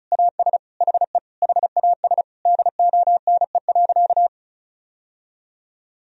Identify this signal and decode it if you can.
Morse code, heard as a keyed tone, about 34 wpm, 700 Hz. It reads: AS HE HAS DONE.
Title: Sentences from Top 1000 Words - Review Part 1 - 35wpm